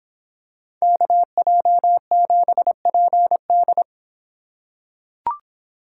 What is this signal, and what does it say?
Morse code, heard as a keyed tone, about 25 wpm, 700 Hz. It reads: KJ7PD